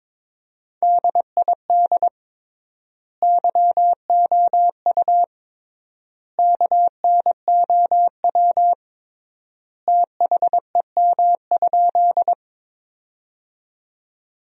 Morse code, heard as a keyed tone, about 22 wpm, 700 Hz. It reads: DID YOU KNOW THEM?
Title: 3x Word-Spacing - Sentences from Top 100 Words - 22wpm